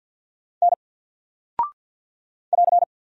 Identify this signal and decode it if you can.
Morse code, heard as a keyed tone, about 48 wpm, 700 Hz. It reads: N <AR>
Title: Single Character (Letter:Number:Punctuation:Prosign) Alt 50wpm